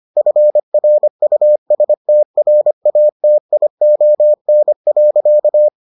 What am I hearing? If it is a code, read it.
Morse code, heard as a keyed tone, about 25 wpm, 600 Hz. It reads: FRUSTRATION.